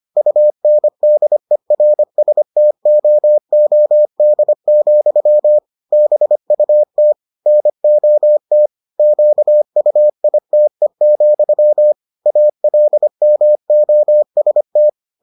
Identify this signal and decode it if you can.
Morse code, heard as a keyed tone, about 25 wpm, 600 Hz. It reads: UNDERSTOOD, BUT NOT QUITE, ALMOST